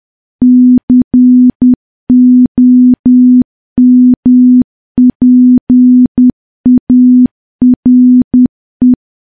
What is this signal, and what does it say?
Morse code, heard as a keyed tone, about 10 wpm, 250 Hz. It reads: COMPARE
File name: morse / morse-training-10WPM